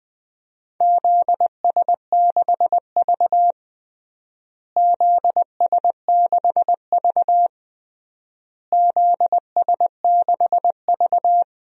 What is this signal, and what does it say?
Morse code, heard as a keyed tone, about 20 wpm, 700 Hz. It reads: ZS6V ZS6V ZS6V
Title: DXCC-100 Call Signs - 4 Characters - Repeated 3x - 20wpm